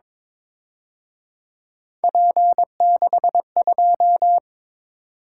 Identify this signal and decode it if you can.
Morse code, heard as a keyed tone, about 27 wpm, 700 Hz. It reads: P62
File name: Speed-Racing - Three Letter-Number Combinations - 22wpm